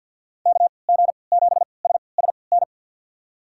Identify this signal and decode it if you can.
Morse code, heard as a keyed tone, about 48 wpm, spaced as for 28 wpm, 700 Hz. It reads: KG7SSN
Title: USA Call Signs - 6 Characters - 3x Character Spacing - 50wpm